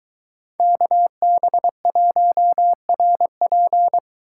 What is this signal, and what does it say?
Morse code, heard as a keyed tone, about 22 wpm, 700 Hz. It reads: KB1RP